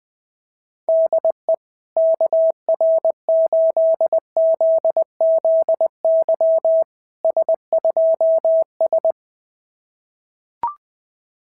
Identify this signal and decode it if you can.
Morse code, heard as a keyed tone, about 20 wpm, 690 Hz. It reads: DE KR8ZZY S2S
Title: SOTA QSO Practice 20wpm